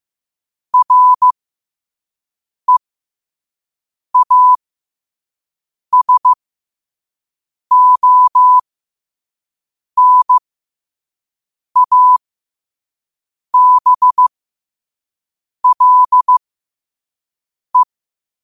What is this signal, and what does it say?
Morse code, heard as a keyed tone, about 15 wpm, spaced as for 5 wpm, 1000 Hz. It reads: REASONABLE
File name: qotdfiles / 5wpm